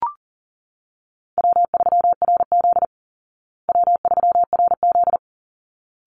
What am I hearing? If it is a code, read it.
Morse code, heard as a keyed tone, about 40 wpm, 700 Hz. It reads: W3RZ W3RZ
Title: Easy Call Signs Repeated 2x - 40wpm